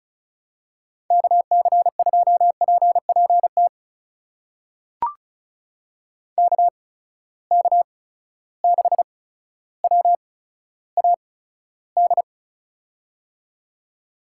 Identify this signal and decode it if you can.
Morse code, heard as a keyed tone, about 36 wpm, spaced as for 15 wpm, 700 Hz. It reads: KC2PPT KK6WAD
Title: USA Call Signs - 6 Characters - 8x Character Spacing - 35wpm